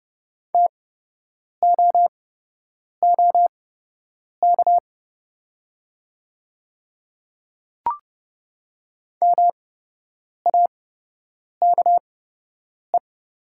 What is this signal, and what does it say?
Morse code, heard as a keyed tone, about 30 wpm, spaced as for 8 wpm, 700 Hz. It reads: TOOK MAKE
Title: 8x Character Spacing - Top 500 Words - 30wpm